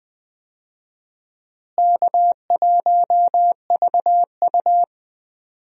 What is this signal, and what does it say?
Morse code, heard as a keyed tone, about 20 wpm, 700 Hz. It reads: K1VU
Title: USA Call Signs - 4 Characters - Repeated 3x - 20wpm